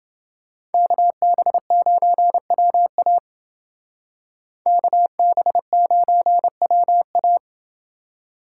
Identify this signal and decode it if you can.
Morse code, heard as a keyed tone, about 28 wpm, 700 Hz. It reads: KB9WA KB9WA